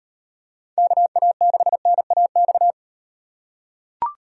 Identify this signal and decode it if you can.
Morse code, heard as a keyed tone, about 36 wpm, 700 Hz. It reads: KA6NAX